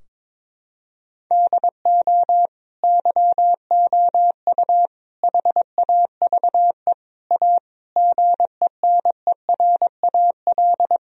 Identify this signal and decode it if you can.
Morse code, heard as a keyed tone, about 22 wpm, 700 Hz. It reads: DO YOU HAVE A GENERAL
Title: Sentences from Top 1000 Words - Review Part 3 - 22wpm